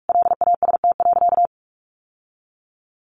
Morse code, heard as a keyed tone, about 44 wpm, 700 Hz. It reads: LAST.
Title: Sentences from Top 1400 Words 45wpm